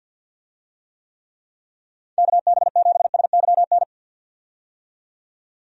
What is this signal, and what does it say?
Morse code, heard as a keyed tone, about 51 wpm, 700 Hz. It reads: KB7SYN
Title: USA Call Signs - 6 Characters - 50wpm